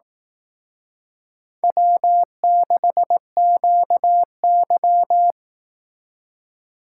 Morse code, heard as a keyed tone, about 18 wpm, 700 Hz. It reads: W6QY
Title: Speed Racing - USA 1x2 Call Signs 15wpm